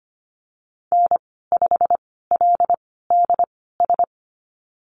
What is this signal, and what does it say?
Morse code, heard as a keyed tone, about 25 wpm, spaced as for 16 wpm, 700 Hz. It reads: N5LDS
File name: Lesson 15g - Number 5 - Call Signs 25wpm